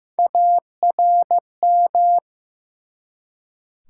Morse code, heard as a keyed tone, about 15 wpm, 700 Hz. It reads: ARM
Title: Three Letter Words - 15wpm